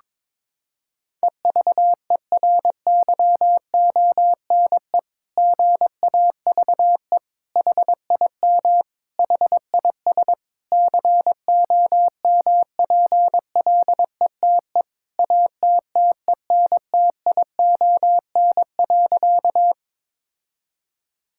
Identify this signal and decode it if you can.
Morse code, heard as a keyed tone, about 22 wpm, 700 Hz. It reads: EVERYONE GAVE HIM HIS COMPLETE ATTENTION.
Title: Sentences from Top 1000 Words - Review Part 6 - 22wpm